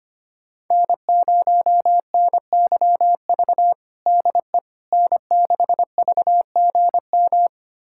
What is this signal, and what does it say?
Morse code, heard as a keyed tone, about 25 wpm, 700 Hz. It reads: N0NYV DE N6VGM
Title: USA Callsign DE Callsign - 5 Characters - 25wpm